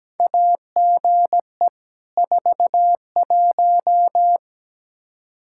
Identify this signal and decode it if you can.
Morse code, heard as a keyed tone, about 17 wpm, 700 Hz. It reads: AGE 41
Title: Practice QSO - Exchange 3 - More Info - 17wpm